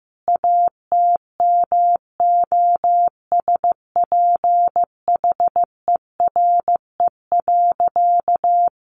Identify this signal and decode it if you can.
Morse code, heard as a keyed tone, about 15 wpm, 700 Hz. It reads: ATMOSPHERE.
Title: A Sound Of Thunder 15wpm